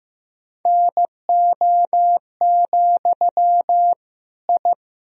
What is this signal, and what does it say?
Morse code, heard as a keyed tone, about 15 wpm, 700 Hz. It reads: NO, I
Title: Sentences from Top 1000 Words - Review Part 2 - 15wpm